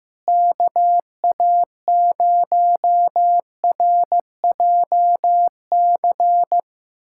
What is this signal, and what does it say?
Morse code, heard as a keyed tone, about 15 wpm, 700 Hz. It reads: KA0RJC
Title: USA Callsign DE Callsign - 6 Characters - 15wpm